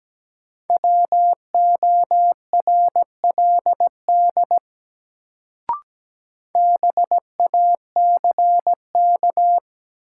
Morse code, heard as a keyed tone, about 17 wpm, 700 Hz. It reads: WORLD BACK
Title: Sentences from Top 200 Words - New Word Drill - 17wpm